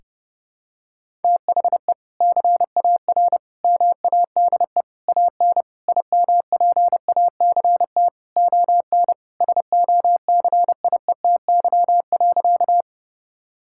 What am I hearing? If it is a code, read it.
Morse code, heard as a keyed tone, about 30 wpm, 700 Hz. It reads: THE CAR MADE AN IMPACT ON SOCIETY.
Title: Sentences from Top 1000 Words - Review Part 6 - 30wpm